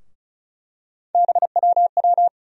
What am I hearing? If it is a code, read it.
Morse code, heard as a keyed tone, about 36 wpm, 700 Hz. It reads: BWW
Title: USA Call Sign Postfixes - 3 Letters 35wpm